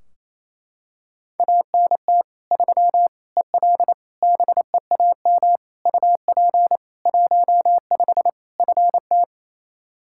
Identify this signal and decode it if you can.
Morse code, heard as a keyed tone, about 28 wpm, 700 Hz. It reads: ANT 3 EL BEAM UP 15 FT
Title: K1USN SST QSO Practice 28wpm